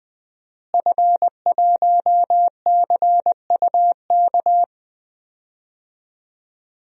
Morse code, heard as a keyed tone, about 20 wpm, 700 Hz. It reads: F1CUK
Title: DXCC-100 Call Signs - 5 Characters - Repeated 3x - 20wpm